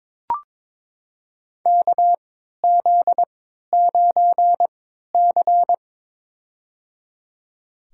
Morse code, heard as a keyed tone, about 22 wpm, spaced as for 13 wpm, 700 Hz. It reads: KZ9C
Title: USA Call Signs - 4 Characters - 3x Character Spacing - 22wpm